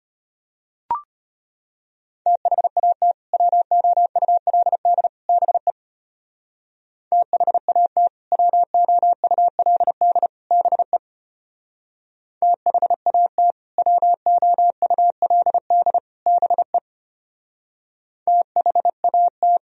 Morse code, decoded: THAT WOULD BE THAT WOULD BE THAT WOULD BE THAT

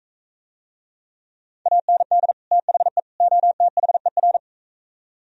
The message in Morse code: AND THE OTHER

